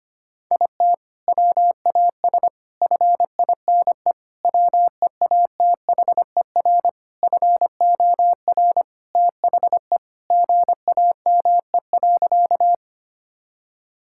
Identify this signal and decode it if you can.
Morse code, heard as a keyed tone, about 25 wpm, 700 Hz. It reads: IT WAS FINE WEATHER FOR THE GAME.
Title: Sentences from Top 1000 Words - Review Part 5 - 25wpm